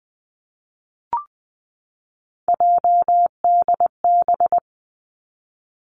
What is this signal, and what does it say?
Morse code, heard as a keyed tone, about 20 wpm, 700 Hz. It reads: JDB
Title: Three Challenging Letter-Number Combinations 20wpm